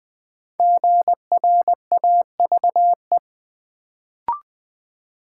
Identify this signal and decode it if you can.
Morse code, heard as a keyed tone, about 20 wpm, 700 Hz. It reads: GRAVE
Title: Build A Word - 1 Letter At A Time - 5-Letter Words - 20wpm